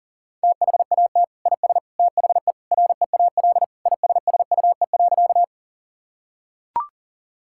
Morse code, heard as a keyed tone, about 37 wpm, 700 Hz. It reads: THAT IS THE REAL ISSUE.
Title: Sentences from Top 1000 Words - Review Part 6 - 40wpm